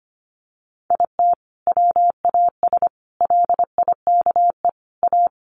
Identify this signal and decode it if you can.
Morse code, heard as a keyed tone, about 25 wpm, 700 Hz. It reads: IT WAS LIKE A
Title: Sentences from Top 1400 Words 25wpm